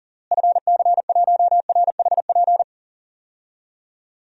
Morse code, heard as a keyed tone, about 40 wpm, 700 Hz. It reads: FC1RHP
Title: DXCC-100 Call Signs - 6 Characters - Repeated 3x - 40wpm